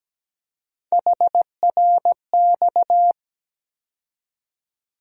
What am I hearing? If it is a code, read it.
Morse code, heard as a keyed tone, about 17 wpm, 700 Hz. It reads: HRX
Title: USA Call Sign Postfixes - 3 Letters 17wpm